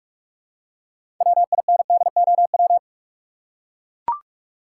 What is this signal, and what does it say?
Morse code, heard as a keyed tone, about 44 wpm, 700 Hz. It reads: WINDOW
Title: Six Letter Words - 45wpm